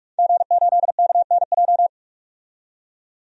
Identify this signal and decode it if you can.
Morse code, heard as a keyed tone, about 42 wpm, 700 Hz. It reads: G8KNJ
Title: DXCC-100 Call Signs - 5 Characters - 45wpm